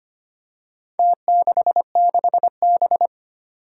Morse code, heard as a keyed tone, about 25 wpm, 700 Hz. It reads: T66B
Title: Four Challenging Letter-Number Combinations 25wpm